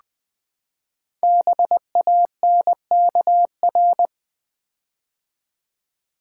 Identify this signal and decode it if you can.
Morse code, heard as a keyed tone, about 20 wpm, 700 Hz. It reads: BANKR
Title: Build A Word - 5 Letters At A Time - 10-Letter Words - 20wpm